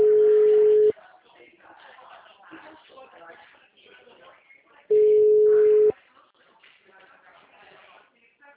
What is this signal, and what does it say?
Ringback tone (Europe, 425 Hz), with speech.
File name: out- -2010- - - .49827